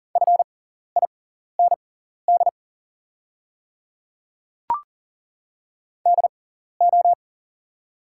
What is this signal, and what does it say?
Morse code, heard as a keyed tone, about 40 wpm, spaced as for 14 wpm, 690 Hz. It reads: FIND DO
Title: 6x Character Spacing - Top 300 Words - 40wpm